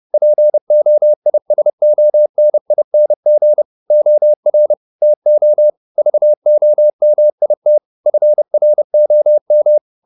Morse code, heard as a keyed tone, about 30 wpm, 600 Hz. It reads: POISONING OR TO VOMIT FROM
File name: the old man and the sea / the old man and the sea 1 30wpm